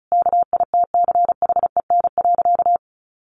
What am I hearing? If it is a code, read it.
Morse code, heard as a keyed tone, about 34 wpm, 700 Hz. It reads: KITCHEN.
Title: Sentences from Top 1400 Words 35wpm